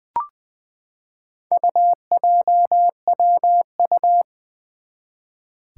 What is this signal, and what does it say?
Morse code, heard as a keyed tone, about 20 wpm, 700 Hz. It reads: UJWU
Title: Four Challenging Letter-Number Combinations 20wpm